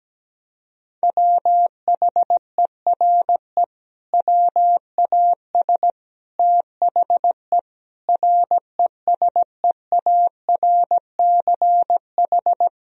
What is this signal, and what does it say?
Morse code, heard as a keyed tone, about 17 wpm, 700 Hz. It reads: WHERE WAS THE RESEARCH
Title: Sentences from Top 1000 Words - Review Part 4 - 17wpm